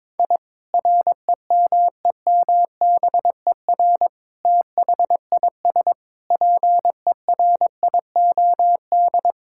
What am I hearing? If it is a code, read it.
Morse code, heard as a keyed tone, about 21 wpm, 700 Hz. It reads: I REMEMBER THIS PERIOD
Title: Sentences from Top 1000 Words - Review Part 5 - 22wpm